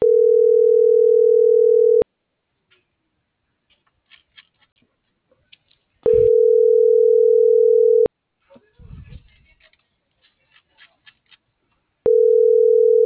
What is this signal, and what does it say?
Ringback tone (North America), 440 + 480 Hz.